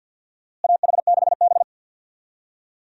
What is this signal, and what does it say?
Morse code, heard as a keyed tone, about 53 wpm, 700 Hz. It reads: AH6B
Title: USA Call Signs - 4 Characters - Repeated 2x - 50wpm